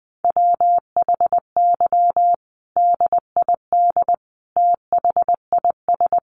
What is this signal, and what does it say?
Morse code, heard as a keyed tone, about 20 wpm, 700 Hz. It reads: WHY DID THIS
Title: Sentences from Top 900 Words 20wpm